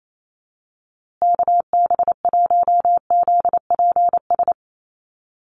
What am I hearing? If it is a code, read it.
Morse code, heard as a keyed tone, about 28 wpm, 700 Hz. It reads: KB1ZPS
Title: USA Call Signs Repeated 2x 28wpm